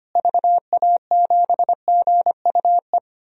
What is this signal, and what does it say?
Morse code, heard as a keyed tone, about 25 wpm, 700 Hz. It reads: VA7GUE
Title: DXCC-100 Call Signs - 6 Characters - Repeated 3x - 25wpm